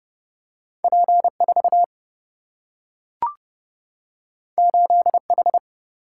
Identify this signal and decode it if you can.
Morse code, heard as a keyed tone, about 30 wpm, 700 Hz. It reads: P 4 8 H
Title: Single-Letter Single-Number 30wpm